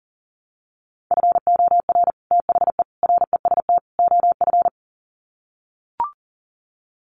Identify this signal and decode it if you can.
Morse code, heard as a keyed tone, about 40 wpm, 700 Hz. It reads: FOR THE REST OF
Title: Sets of 4 Words 40wpm